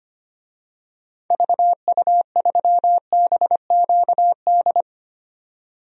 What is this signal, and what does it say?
Morse code, heard as a keyed tone, about 25 wpm, 700 Hz. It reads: VU3BQD
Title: DXCC-100 Call Signs - 6 Characters - 25wpm